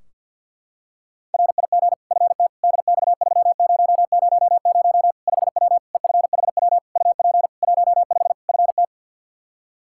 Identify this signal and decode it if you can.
Morse code, heard as a keyed tone, about 51 wpm, 700 Hz. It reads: RIG FT DX3000 5W EFHW UP 15 FT
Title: K1USN SST QSO Practice 50wpm